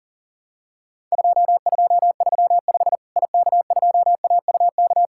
Morse code, heard as a keyed tone, about 40 wpm, 700 Hz. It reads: 2235 IK2AUK
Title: Worked All Europe DX Contest - QTC Exchange - 40wpm